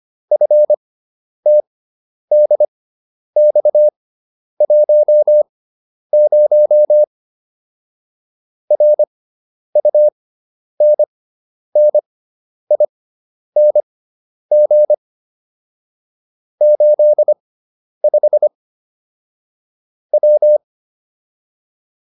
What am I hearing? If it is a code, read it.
Morse code, heard as a keyed tone, about 25 wpm, spaced as for 10 wpm, 600 Hz. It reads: FTDX10 RUNNING 85 W
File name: QSO303 10